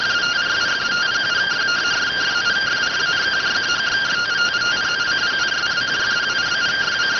Two-tone teleprinter signal, about 170 Hz shift.